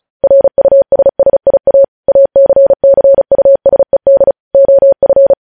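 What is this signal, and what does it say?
Morse code, heard as a keyed tone, about 36 wpm, 550 Hz. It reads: RUSSIA ACCUSED OF
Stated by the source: Morse news headlines sent at 35.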